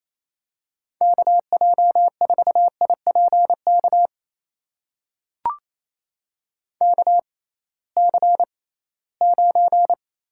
Morse code, heard as a keyed tone, about 27 wpm, spaced as for 18 wpm, 700 Hz. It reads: KJ4IPK KC9
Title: USA Call Signs - 6 Characters - 6x Character Spacing - 28wpm